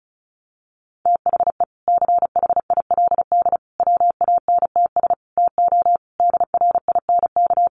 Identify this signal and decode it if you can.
Morse code, heard as a keyed tone, about 34 wpm, 700 Hz. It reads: THE CHILD WANTS TO DRINK
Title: Additional Easy Phrases 35wpm